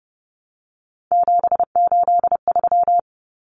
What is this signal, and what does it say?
Morse code, heard as a keyed tone, about 30 wpm, 700 Hz. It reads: 783
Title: Common Cut Numbers - 3-Digit 30wpm